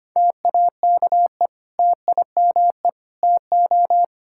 Morse code, heard as a keyed tone, about 25 wpm, 700 Hz. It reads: TAKE TIME TO
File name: practice / 25wpm / Sentences with 4-Word Phrases - 25wpm